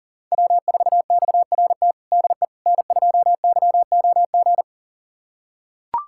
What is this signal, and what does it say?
Morse code, heard as a keyed tone, about 37 wpm, 700 Hz. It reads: W4XRT DE N2YOG
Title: USA Callsign DE Callsign 40wpm